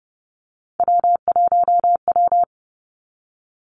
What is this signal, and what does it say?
Morse code, heard as a keyed tone, about 30 wpm, 700 Hz. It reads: W1W
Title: Three Challenging Letter-Number Combinations 30wpm